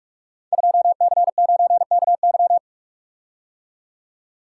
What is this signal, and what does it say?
Morse code, heard as a keyed tone, about 44 wpm, 700 Hz. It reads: 2C9KY
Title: Code Groups - 5 Letter:Numbers - 45wpm